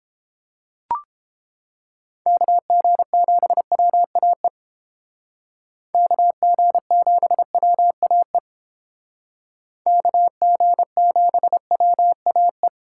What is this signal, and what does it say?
Morse code, heard as a keyed tone, about 31 wpm, 700 Hz. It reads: KG7WAE KG7WAE KG7WAE